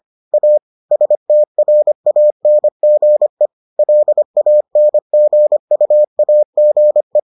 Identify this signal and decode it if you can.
Morse code, heard as a keyed tone, about 25 wpm, 600 Hz. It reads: A STRANGE LANGUAGE